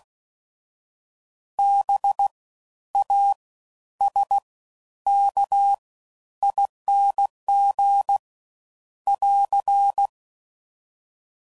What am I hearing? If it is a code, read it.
Morse code, heard as a keyed tone, about 16 wpm, spaced as for 10 wpm, 800 Hz. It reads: BASKING <AR>
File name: Suffixes / Special INGs / Fast Ings